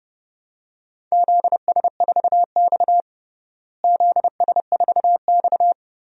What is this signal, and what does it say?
Morse code, heard as a keyed tone, about 30 wpm, 700 Hz. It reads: ZS4X ZS4X
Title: DXCC-100 Call Signs - 4 Characters - Repeated 2x - 30wpm